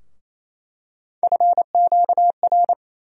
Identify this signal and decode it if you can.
Morse code, heard as a keyed tone, about 28 wpm, 700 Hz. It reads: FQR
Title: USA Call Sign Postfixes - 3 Letters 28wpm